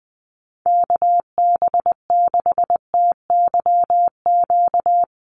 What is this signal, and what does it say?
Morse code, heard as a keyed tone, about 20 wpm, 700 Hz. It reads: KB6TYQ
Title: USA Call Signs Repeated 2x 20wpm